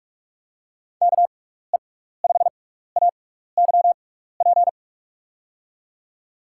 Morse code, heard as a keyed tone, about 46 wpm, spaced as for 16 wpm, 700 Hz. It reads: KE5AYP
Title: USA Call Signs - 6 Characters - 6x Character Spacing - 45wpm